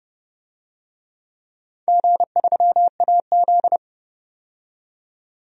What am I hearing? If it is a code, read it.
Morse code, heard as a keyed tone, about 30 wpm, 700 Hz. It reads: G3AZ
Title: DXCC-100 Call Signs - 4 Characters - 30wpm